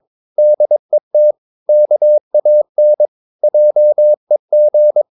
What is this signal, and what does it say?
Morse code, heard as a keyed tone, about 22 wpm, 600 Hz. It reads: DET KAN JEG
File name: Ferie QRS 22wpm